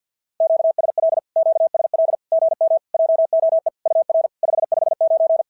Morse code, heard as a keyed tone, about 51 wpm, 650 Hz. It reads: QSL QSL GM JOE UR 559
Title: SOTA QSO Practice 50wpm